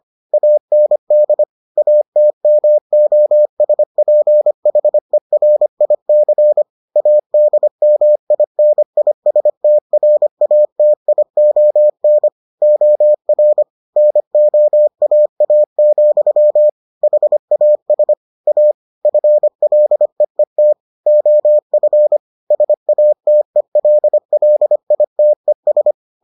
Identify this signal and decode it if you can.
Morse code, heard as a keyed tone, about 25 wpm, 600 Hz. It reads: AND ATMOSPHERIC ADMINISTRATION OR NOAA, HAS A FLEET OF SATELLITES